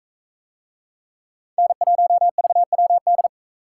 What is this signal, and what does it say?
Morse code, heard as a keyed tone, about 44 wpm, 700 Hz. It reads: N1VWD